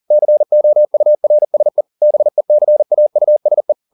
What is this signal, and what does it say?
Morse code, heard as a keyed tone, about 40 wpm, 600 Hz. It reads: COURSE BECAUSE
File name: the old man and the sea 2 40wpm